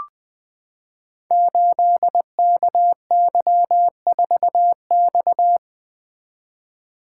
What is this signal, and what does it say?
Morse code, heard as a keyed tone, about 20 wpm, 700 Hz. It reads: 8KY4X
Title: Code Groups - 5 Letter:Numbers - 20wpm